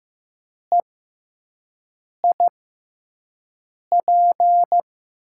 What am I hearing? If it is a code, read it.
Morse code, heard as a keyed tone, about 15 wpm, spaced as for 5 wpm, 700 Hz. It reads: EIP
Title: USA Call Signs - 5 Characters - 6x Character Spacing - 15wpm